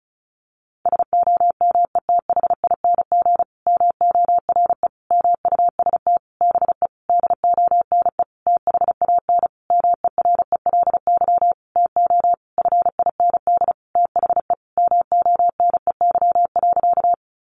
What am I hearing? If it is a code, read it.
Morse code, heard as a keyed tone, about 34 wpm, 700 Hz. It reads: SOMETHING MORE MUST BE DONE THAN MERELY TO FIND THE MONEY.
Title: Sentences from Top 1400 Words 35wpm